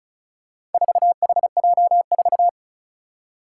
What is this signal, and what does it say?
Morse code, heard as a keyed tone, about 36 wpm, 700 Hz. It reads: 4HJ4
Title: Four Challenging Letter-Number Combinations 35wpm